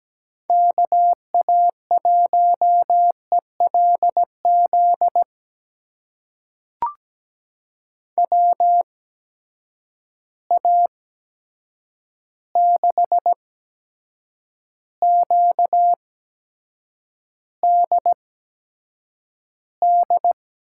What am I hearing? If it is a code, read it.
Morse code, heard as a keyed tone, about 17 wpm, spaced as for 7 wpm, 700 Hz. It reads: KA1ELZ WA6QDD
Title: USA Call Signs - 6 Characters - 8x Character Spacing - 17wpm